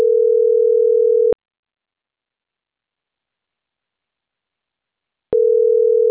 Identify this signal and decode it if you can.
Ringback tone (North America), 440 + 480 Hz.